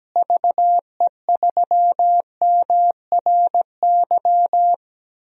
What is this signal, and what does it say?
Morse code, heard as a keyed tone, about 17 wpm, 700 Hz. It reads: VE3MRY
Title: DXCC-100 Call Signs - 6 Characters - Repeated 3x - 17wpm